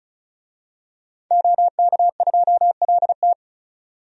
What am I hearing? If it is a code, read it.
Morse code, heard as a keyed tone, about 34 wpm, 700 Hz. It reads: OK2LT
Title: DXCC-100 Call Signs - 5 Characters - 35wpm